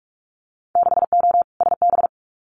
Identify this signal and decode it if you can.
Morse code, heard as a keyed tone, about 46 wpm, 700 Hz. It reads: 6O SB
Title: Winter Field Day Exchanges Encore 45wpm